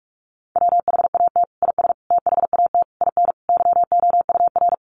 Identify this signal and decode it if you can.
Morse code, heard as a keyed tone, about 44 wpm, 700 Hz. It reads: WHAT IS THAT IN YOUR
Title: Sentences from Top 200 Words 45wpm